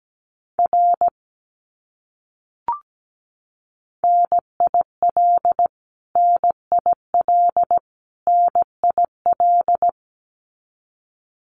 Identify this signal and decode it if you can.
Morse code, heard as a keyed tone, about 17 wpm, 700 Hz. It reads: R NIL NIL NIL
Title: Abbreviations Repeated 3x - 17wpm